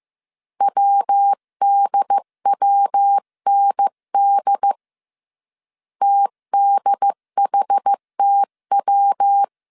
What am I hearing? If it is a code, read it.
Morse code, heard as a keyed tone, about 15 wpm, 800 Hz. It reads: WDWND TDHTW
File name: CW / DET / 1 3 1500C S A 14wpm 2447